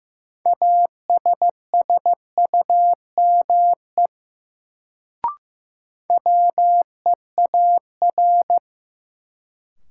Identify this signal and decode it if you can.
Morse code, heard as a keyed tone, about 15 wpm, 700 Hz. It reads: ASSUME WEAR
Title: Sentences from Top 1500 Words - New Word Drill Alt - 15wpm